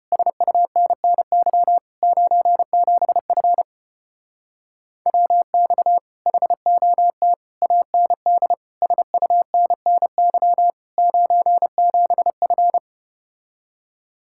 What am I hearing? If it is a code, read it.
Morse code, heard as a keyed tone, about 33 wpm, 700 Hz. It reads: SUNNY 97F WX HOT AND SUNNY 97F